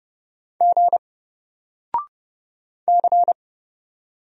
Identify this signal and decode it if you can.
Morse code, heard as a keyed tone, about 30 wpm, 700 Hz. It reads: G C
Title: Single Character (Letter:Number:Punctuation:Prosign) 30wpm